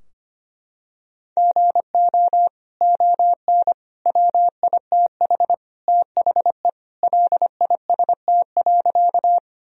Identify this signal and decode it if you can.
Morse code, heard as a keyed tone, about 25 wpm, 700 Hz. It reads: GO ON WITH THE LIST.
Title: Sentences from Top 1000 Words - Review Part 2 - 25wpm